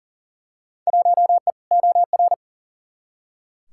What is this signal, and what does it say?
Morse code, heard as a keyed tone, about 40 wpm, 700 Hz. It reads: 1E OR
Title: Field Day Exchanges Encore - 40wpm